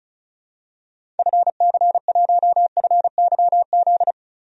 Morse code, heard as a keyed tone, about 36 wpm, 700 Hz. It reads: FC1FYZ
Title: DXCC-100 Call Signs - 6 Characters - 35wpm